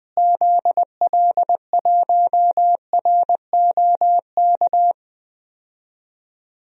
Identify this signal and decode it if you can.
Morse code, heard as a keyed tone, about 20 wpm, 700 Hz. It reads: ZL1ROK
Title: DXCC-100 Call Signs - 6 Characters - Repeated 3x - 20wpm